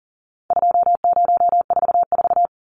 Morse code, heard as a keyed tone, about 40 wpm, 700 Hz. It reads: 2044
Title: Numbers - 4-Digit 40wpm